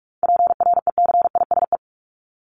Morse code, heard as a keyed tone, about 44 wpm, 700 Hz. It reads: PRECISE
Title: LY Suffix - Top 100 Words with Base Words 45wpm